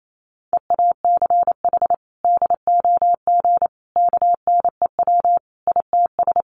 Morse code, heard as a keyed tone, about 28 wpm, 700 Hz. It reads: EACH DOG KNEW ITS
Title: Sentences from Top 1300 Words 28wpm